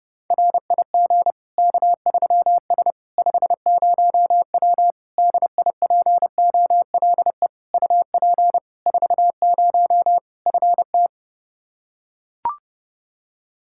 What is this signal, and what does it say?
Morse code, heard as a keyed tone, about 30 wpm, 700 Hz. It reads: RIG K3S 50W DIPOLE UP 40 FT
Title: K1USN SST QSO Practice 30wpm